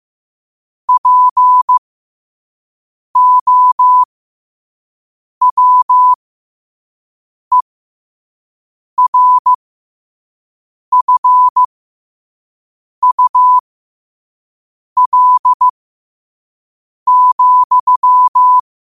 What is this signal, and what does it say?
Morse code, heard as a keyed tone, about 15 wpm, spaced as for 5 wpm, 1000 Hz. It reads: POWERFUL,
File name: qotdfiles / 5wpm